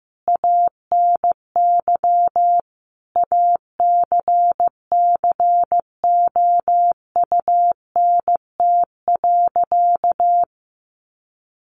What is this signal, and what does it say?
Morse code, heard as a keyed tone, about 14 wpm, 700 Hz. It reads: ANY ACCOUNT.
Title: Sentences from Top 900 Words 15wpm